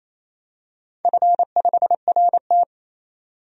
Morse code, heard as a keyed tone, about 27 wpm, 700 Hz. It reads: F5RT
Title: DXCC-100 Call Signs - 4 Characters - 28wpm